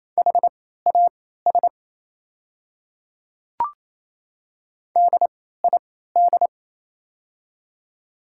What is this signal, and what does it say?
Morse code, heard as a keyed tone, about 27 wpm, spaced as for 16 wpm, 700 Hz. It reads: HAS DID
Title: 3x Character Spacing - Top 100 Words - 28wpm